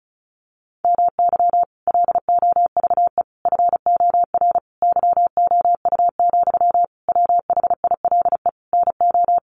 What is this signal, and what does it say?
Morse code, heard as a keyed tone, about 34 wpm, 700 Hz. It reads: MY LOVE FOR YOU, WHILE NO
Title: Sentences from Top 1400 Words 35wpm